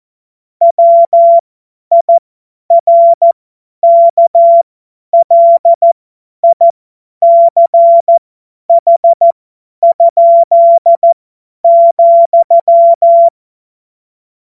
Morse code, heard as a keyed tone, about 14 wpm, spaced as for 10 wpm, 680 Hz. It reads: WIRKLICH?,
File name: Morsekurs / Witze / Witz 31. Januar 2023 50 F70